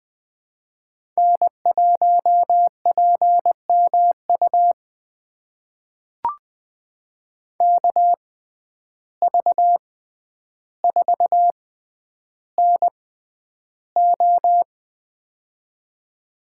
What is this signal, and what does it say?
Morse code, heard as a keyed tone, about 20 wpm, spaced as for 10 wpm, 700 Hz. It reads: N1PMU KV4NO
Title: USA Call Signs - 5 Characters - 6x Character Spacing - 20wpm